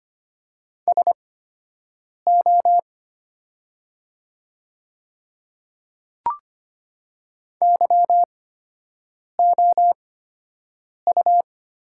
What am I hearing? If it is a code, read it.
Morse code, heard as a keyed tone, about 25 wpm, spaced as for 7 wpm, 700 Hz. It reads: SO YOU
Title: 8x Character Spacing - Top 100 Words - 25wpm